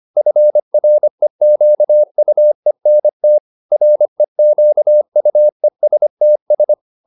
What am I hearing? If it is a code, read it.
Morse code, heard as a keyed tone, about 25 wpm, 600 Hz. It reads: FREQUENT REQUESTS